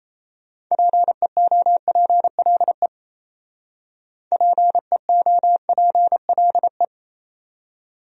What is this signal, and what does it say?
Morse code, heard as a keyed tone, about 33 wpm, 700 Hz. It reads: PEOPLE PEOPLE
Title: Speed-Racing Top 200 Words 28wpm